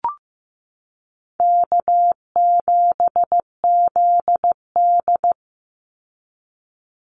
Morse code, heard as a keyed tone, about 15 wpm, 700 Hz. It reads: K7ZD
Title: Easy Call Signs Repeated 2x - 15wpm